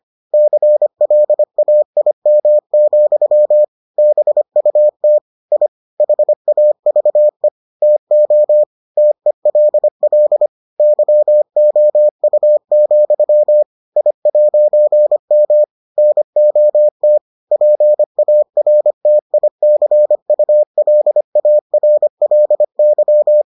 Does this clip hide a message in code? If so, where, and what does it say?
Yes: Morse code from 0.3 to 23.4 s, heard as a keyed tone, about 25 wpm, 600 Hz. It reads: CLAIM, BUT I HAVE TO TELL YOU, I'M NOT PARTICULARLY